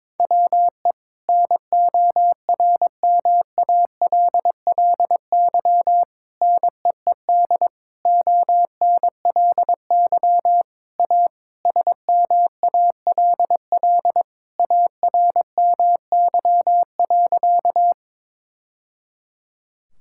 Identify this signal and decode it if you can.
Morse code, heard as a keyed tone, about 22 wpm, 700 Hz. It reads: WE NORMALLY NEED ONLY A SMALL ARMY.
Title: Sentences from Top 1000 Words - Review Part 6 - 22wpm